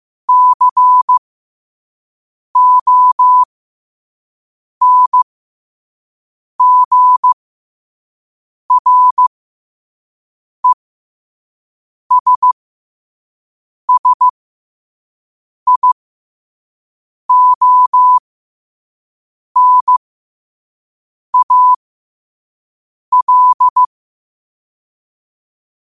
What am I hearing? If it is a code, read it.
Morse code, heard as a keyed tone, about 15 wpm, spaced as for 5 wpm, 1000 Hz. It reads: CONGRESSIONAL
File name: qotdfiles / 5wpm